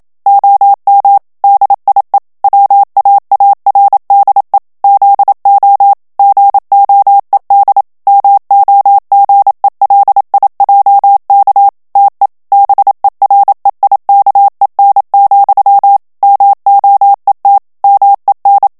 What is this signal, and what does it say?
Morse code, heard as a keyed tone, about 27 wpm, 800 Hz. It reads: OM DIE WAARDE ZO GOED MOGELIJK TE BEREIKEN, MOET MEN
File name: HOLLANDAIS texte clair 1 vitesse 26 mots minute